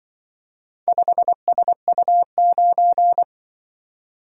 Morse code, heard as a keyed tone, about 24 wpm, 700 Hz. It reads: 5SU9
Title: Speed-Racing - Four Letter-Number Combinations - 20wpm